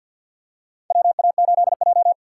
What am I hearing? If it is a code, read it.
Morse code, heard as a keyed tone, about 49 wpm, 700 Hz. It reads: WA8J
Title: USA Call Signs - 4 Characters - 2x Character Spacing - 50wpm